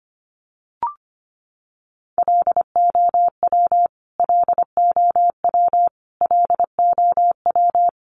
Morse code, heard as a keyed tone, about 25 wpm, 700 Hz. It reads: LOW LOW LOW
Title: Top 500 Words Repeated 25wpm